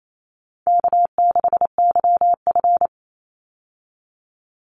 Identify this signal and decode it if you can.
Morse code, heard as a keyed tone, about 28 wpm, 700 Hz. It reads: K6YF
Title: Easy Call Signs Repeated 2x - 28wpm